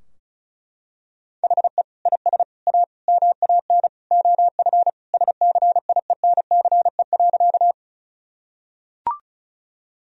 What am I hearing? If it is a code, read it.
Morse code, heard as a keyed tone, about 34 wpm, 700 Hz. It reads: HE IS A MAN OF SCIENCE.
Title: Sentences from Top 1000 Words - Review Part 2 - 35wpm